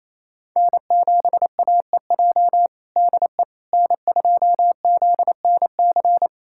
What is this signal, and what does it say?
Morse code, heard as a keyed tone, about 28 wpm, 700 Hz. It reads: N7AEJ DE N2ZNC
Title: USA Callsign DE Callsign 28wpm